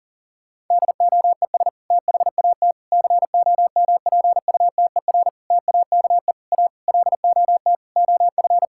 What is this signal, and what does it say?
Morse code, heard as a keyed tone, about 37 wpm, 700 Hz. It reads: DOES THAT COMPUTER TAKE A LOT OF
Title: Sentences from Top 1000 Words - Review Part 6 - 40wpm